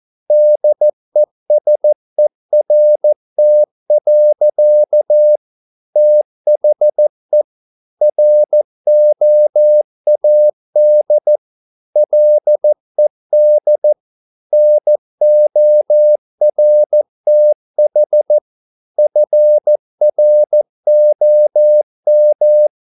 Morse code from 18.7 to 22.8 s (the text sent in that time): FROM